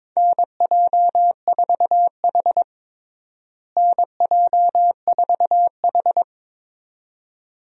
Morse code, heard as a keyed tone, about 22 wpm, 700 Hz. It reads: NJ4H NJ4H
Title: USA Call Signs - 4 Characters - Repeated 2x - 22wpm